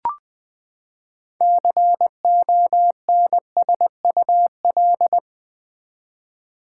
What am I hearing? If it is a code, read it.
Morse code, heard as a keyed tone, about 20 wpm, 700 Hz. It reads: CONSUL
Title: Build A Word - 6 Letters At A Time - 12-Letter Words - 20wpm